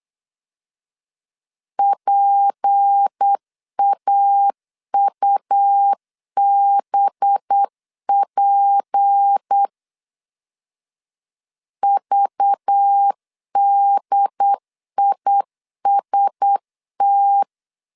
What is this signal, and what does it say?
Morse code, heard as a keyed tone, about 8 wpm, 800 Hz. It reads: PAUBP VDIST
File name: LEZIONI / 6 PKX0 / 6 P 8wpm 3025